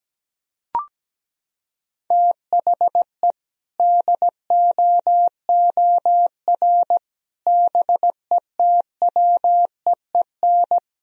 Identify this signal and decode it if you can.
Morse code, heard as a keyed tone, about 17 wpm, 700 Hz. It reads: THE DOOR BETWEEN
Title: Sentences from Top 1000 Words - Review Part 1 - 17wpm